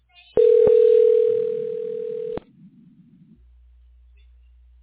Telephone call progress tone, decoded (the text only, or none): ringback tone